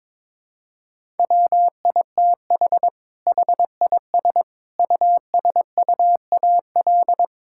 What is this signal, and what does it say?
Morse code, heard as a keyed tone, about 22 wpm, 700 Hz. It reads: WITH HIS USUAL
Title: Sentences from Top 1000 Words - Review Part 6 - 22wpm